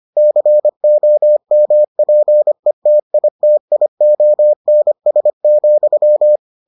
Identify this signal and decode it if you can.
Morse code, heard as a keyed tone, about 25 wpm, 600 Hz. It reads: COMPETITIONS,